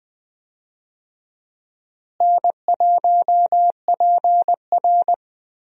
Morse code, heard as a keyed tone, about 20 wpm, 700 Hz. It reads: N1PR
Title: USA Call Signs - 4 Characters - Repeated 2x - 20wpm